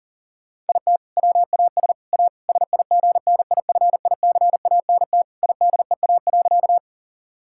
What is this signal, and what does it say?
Morse code, heard as a keyed tone, about 40 wpm, 700 Hz. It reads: IT WAS A SIGNIFICANT IDEA.
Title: Sentences from Top 1000 Words - Review Part 6 - 40wpm